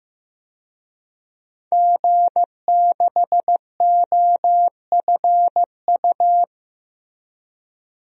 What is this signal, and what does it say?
Morse code, heard as a keyed tone, about 16 wpm, 700 Hz. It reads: G6OFU
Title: DXCC-100 Call Signs - 5 Characters - Repeated 2x - 15wpm